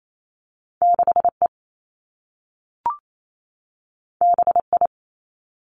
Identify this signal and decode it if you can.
Morse code, heard as a keyed tone, about 28 wpm, 700 Hz. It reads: 6 E B I
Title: Two Challenging Letter-Number Combinations 28wpm